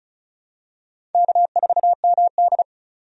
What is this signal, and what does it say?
Morse code, heard as a keyed tone, about 36 wpm, 700 Hz. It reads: K4MD
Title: USA Call Signs - 4 Characters - 35wpm